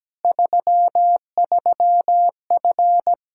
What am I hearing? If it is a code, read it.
Morse code, heard as a keyed tone, about 17 wpm, 700 Hz. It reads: 33F
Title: Practice QSO - Exchange 2 - Info - 17wpm